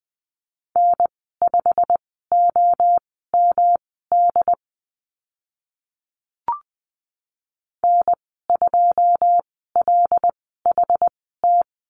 Morse code, heard as a keyed tone, about 20 wpm, spaced as for 15 wpm, 700 Hz. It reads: N5OMD N2LHT
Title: 128 Lesson 21g - Number 6 - Call Signs 20wpm